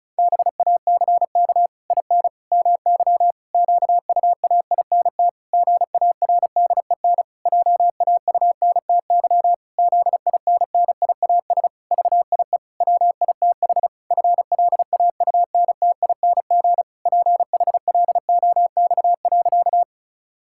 Morse code, heard as a keyed tone, about 36 wpm, 700 Hz. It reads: BACK IN MY QUAINT GARDEN JAUNTY ZINNIAS VIE WITH FLAUNTING PHLOX.
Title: Pangram Challenge v2 - 35wpm